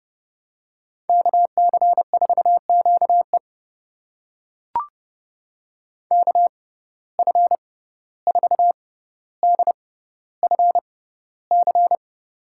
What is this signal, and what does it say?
Morse code, heard as a keyed tone, about 30 wpm, spaced as for 15 wpm, 700 Hz. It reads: KC4QE KF4DFC
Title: USA Call Signs - 6x Character Spacing - 30wpm